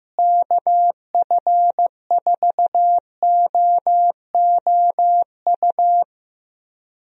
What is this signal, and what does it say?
Morse code, heard as a keyed tone, about 15 wpm, 700 Hz. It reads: KF4OOU
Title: USA Call Signs - 6 Characters - Repeated 2x - 15wpm